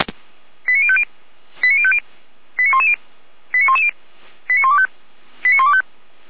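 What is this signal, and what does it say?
Five-tone selective call (ZVEI), heard as quick runs of five tones, tones of about 70 ms. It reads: 89050 89050 89110 89110 89125 89125
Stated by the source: Five-tone ZVEI selective call received on an SDR.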